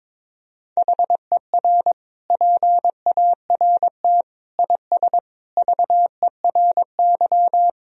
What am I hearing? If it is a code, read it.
Morse code, heard as a keyed tone, about 22 wpm, 700 Hz. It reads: HER PART IS VERY
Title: Sentences from Top 1000 Words - Review Part 1 - 22wpm